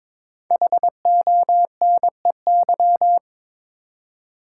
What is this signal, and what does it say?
Morse code, heard as a keyed tone, about 22 wpm, 700 Hz. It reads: HONEY
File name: practice / 22wpm / 5 Letter Words - First 3 Letters Given - 22wpm